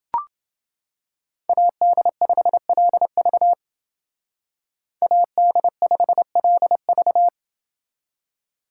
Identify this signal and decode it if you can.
Morse code, heard as a keyed tone, about 29 wpm, 700 Hz. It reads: AD5LV AD5LV